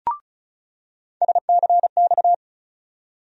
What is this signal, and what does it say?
Morse code, heard as a keyed tone, about 36 wpm, 700 Hz. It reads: SCX
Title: USA Call Sign Postfixes - 3 Letters 35wpm